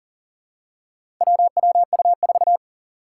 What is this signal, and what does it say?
Morse code, heard as a keyed tone, about 37 wpm, 700 Hz. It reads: WWU4
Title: Four Challenging Letter-Number Combinations 40wpm